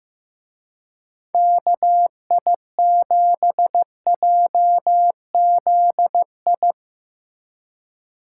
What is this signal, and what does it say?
Morse code, heard as a keyed tone, about 17 wpm, 700 Hz. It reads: KI7JZI
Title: Speed Racing - USA 2x3 Call Signs 15wpm